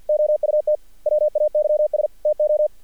Morse code, heard as a keyed tone, about 48 wpm, 600 Hz. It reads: OUT WAYS TO
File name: archived / CRSnet